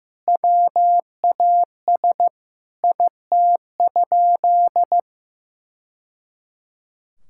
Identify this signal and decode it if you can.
Morse code, heard as a keyed tone, about 15 wpm, 700 Hz. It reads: WAS IT?
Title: Sentences from Top 1000 Words - Review Part 6 - 15wpm